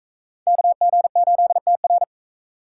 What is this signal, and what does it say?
Morse code, heard as a keyed tone, about 38 wpm, 700 Hz. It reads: KG8TR